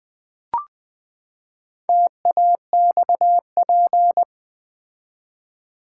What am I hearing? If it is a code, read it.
Morse code, heard as a keyed tone, about 20 wpm, 700 Hz. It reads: TAXP
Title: Build A Word - 4 Letters At A Time - 8-Letter Words - 20wpm